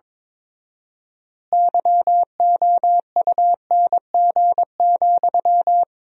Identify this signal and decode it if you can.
Morse code, heard as a keyed tone, about 22 wpm, 700 Hz. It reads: YOUNG,
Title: 4x Word-Spacing - Sentences from Top 400 Words - 22wpm